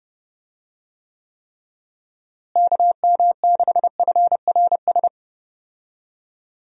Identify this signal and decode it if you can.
Morse code, heard as a keyed tone, about 29 wpm, 700 Hz. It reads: KM6FRS
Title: USA Call Signs - 6 Characters - Repeated 3x - 30wpm